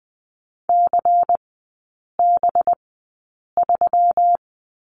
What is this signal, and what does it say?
Morse code, heard as a keyed tone, about 20 wpm, spaced as for 8 wpm, 700 Hz. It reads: CB3
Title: 184 Lesson 30e - Letter b - Three Characters 20wpm